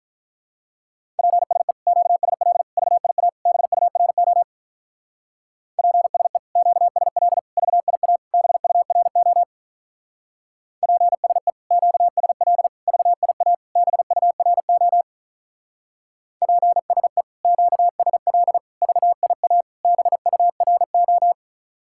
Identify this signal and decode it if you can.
Morse code, heard as a keyed tone, about 44 wpm, 700 Hz. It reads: PSE QSL VIA BURO PSE QSL VIA BURO PSE QSL VIA BURO PSE QSL VIA BURO